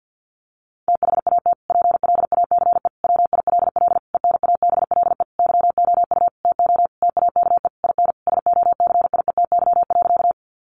Morse code, heard as a keyed tone, about 50 wpm, 700 Hz. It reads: THAT PLACE WILL ENABLE YOU TO TAKE IN SOCIETY.
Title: Sentences from Top 1500 Words 50wpm